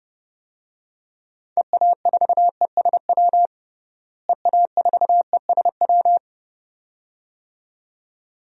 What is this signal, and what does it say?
Morse code, heard as a keyed tone, about 30 wpm, 700 Hz. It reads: EA4ESW EA4ESW
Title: DXCC-100 Call Signs - 6 Characters - Repeated 2x - 30wpm